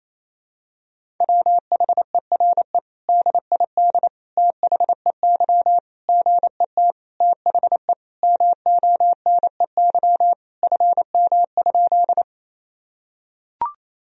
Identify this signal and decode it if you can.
Morse code, heard as a keyed tone, about 27 wpm, 700 Hz. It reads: WHERE DID THEY GET THE MONEY FM?
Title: Sentences from Top 400 Words and Abbreviations 28wpm